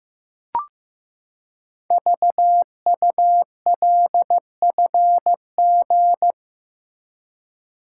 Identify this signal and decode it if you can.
Morse code, heard as a keyed tone, about 15 wpm, 700 Hz. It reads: VULFG
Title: Code Groups - 5 Letters - 15wpm